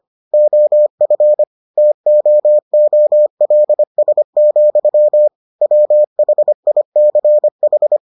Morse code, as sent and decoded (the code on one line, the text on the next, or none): --- ..-. / - --- --- .-.. ... --..-- / .-- .... .. -.-. ....
OF TOOLS, WHICH